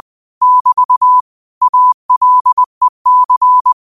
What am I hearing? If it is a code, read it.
Morse code, heard as a keyed tone, about 20 wpm, 1000 Hz. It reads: <BT> ALEC
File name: qotdfiles / 20wpm